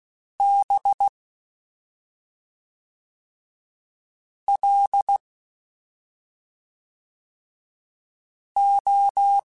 Morse code, heard as a keyed tone, about 16 wpm, 800 Hz. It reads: B L O
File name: Special INGs / Special INGs-mp3s / SlowToFaster ing @5